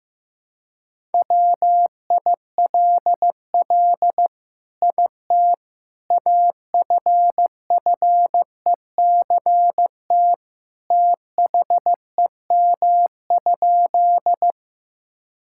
Morse code, heard as a keyed tone, about 15 wpm, 700 Hz. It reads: WILL IT AFFECT THEM?
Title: Sentences from Top 1000 Words - Review Part 6 - 15wpm